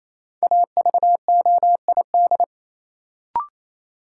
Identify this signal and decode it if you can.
Morse code, heard as a keyed tone, about 28 wpm, 700 Hz. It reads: AVOID
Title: Sentences from Top 1200 Words - New Word Drill Alt - 28wpm